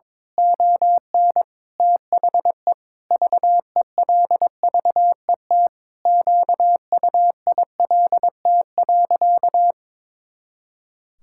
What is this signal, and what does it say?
Morse code, heard as a keyed tone, about 22 wpm, 700 Hz. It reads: ON THE VELVET QUILT.
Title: Pangram Challenge v2 - 22wpm